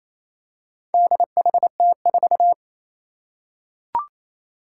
Morse code, heard as a keyed tone, about 28 wpm, 700 Hz. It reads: DHT4
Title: Four Challenging Letter-Number Combinations 28wpm